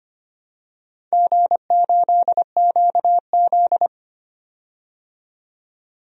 Morse code, heard as a keyed tone, about 25 wpm, 700 Hz. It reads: G8QZ
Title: DXCC-100 Call Signs - 4 Characters - 25wpm